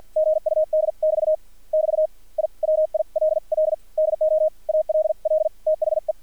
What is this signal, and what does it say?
Morse code, heard as a keyed tone, about 50 wpm, spaced as for 43 wpm, 620 Hz. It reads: GUD<BT> <BT> I WILL DO ALL THE
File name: archived / CRSnet